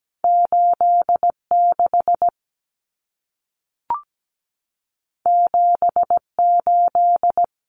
Morse code, heard as a keyed tone, about 17 wpm, 700 Hz. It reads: 86 78
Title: Numbers - 2-Digit 17wpm